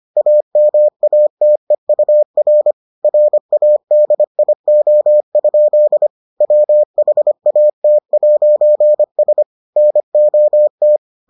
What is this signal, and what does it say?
Morse code, heard as a keyed tone, about 25 wpm, 600 Hz. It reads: AMATEUR RADIO? WHAT'S NOT